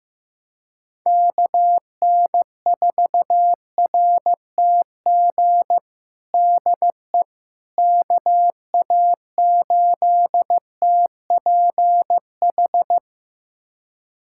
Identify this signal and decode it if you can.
Morse code, heard as a keyed tone, about 15 wpm, 700 Hz. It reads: KN4RTG DE KA8TPH
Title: USA Callsign DE Callsign - 6 Characters - 15wpm